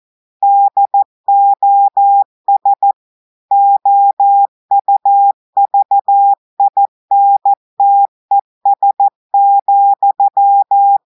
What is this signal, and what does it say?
Morse code, heard as a keyed tone, about 14 wpm, 800 Hz. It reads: DOS OUVINTES,
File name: audios / portugais / portugais texte en clair 2 vitesse 14 mots minutes